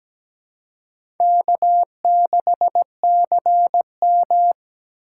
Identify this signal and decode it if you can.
Morse code, heard as a keyed tone, about 17 wpm, 700 Hz. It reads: K6CM
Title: USA Call Signs - 4 Characters - 4x Character Spacing - 17wpm